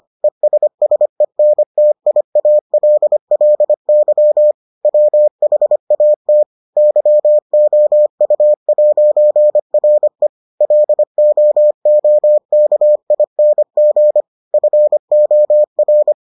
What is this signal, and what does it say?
Morse code, heard as a keyed tone, about 25 wpm, 600 Hz. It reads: ESSENTIALLY WHAT YOU'RE LOOKING FOR